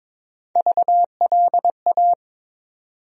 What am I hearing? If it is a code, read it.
Morse code, heard as a keyed tone, about 22 wpm, 700 Hz. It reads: VLA
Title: USA Call Sign Postfixes - 3 Letters 22wpm